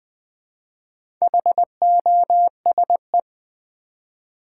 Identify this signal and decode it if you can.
Morse code, heard as a keyed tone, about 20 wpm, 700 Hz. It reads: HOSE
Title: Easy Words with Hints - 20wpm